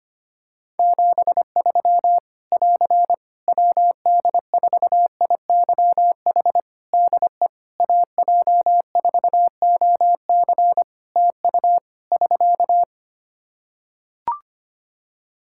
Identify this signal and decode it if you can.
Morse code, heard as a keyed tone, about 25 wpm, 700 Hz. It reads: 73 <AR> WD4IYH DE AJ4OC TU <SK>
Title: Practice QSO - Exchange 4 - Ending - 25wpm